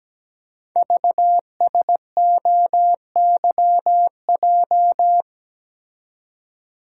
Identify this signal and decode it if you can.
Morse code, heard as a keyed tone, about 17 wpm, 700 Hz. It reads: VSOYJ
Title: Code Groups - 5 Letters - 17wpm